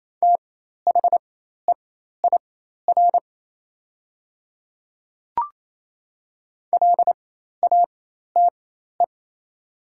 Morse code, heard as a keyed tone, about 28 wpm, spaced as for 13 wpm, 700 Hz. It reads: THEIR LATE
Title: 4x Character Spacing - Top 300 Words - 28wpm